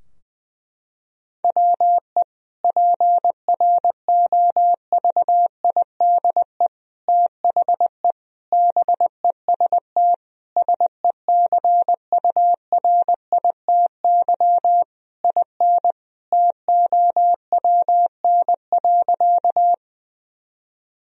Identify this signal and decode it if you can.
Morse code, heard as a keyed tone, about 20 wpm, 700 Hz. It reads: WE PROVIDE THE BEST SECURITY IN TOWN.
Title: Sentences from Top 1000 Words - Review Part 4 - 20wpm